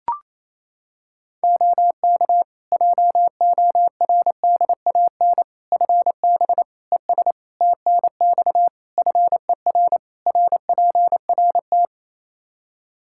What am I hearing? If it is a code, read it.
Morse code, heard as a keyed tone, about 28 wpm, 700 Hz. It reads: OK JORDAN FB ES TNX FER RPRT
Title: Practice QSO - Exchange 2 - Info - 28wpm